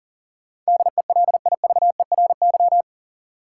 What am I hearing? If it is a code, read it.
Morse code, heard as a keyed tone, about 38 wpm, 700 Hz. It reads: DELIVERY
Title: Sentences from Top 1500 Words - New Word Drill - 40wpm